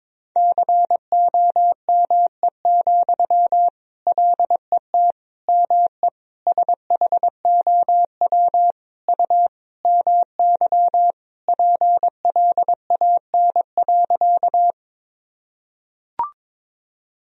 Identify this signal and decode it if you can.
Morse code, heard as a keyed tone, about 23 wpm, 700 Hz. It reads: COME, LET ME SHOW U MY PLAN.
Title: Sentences from Top 500 Words and Abbreviations 22wpm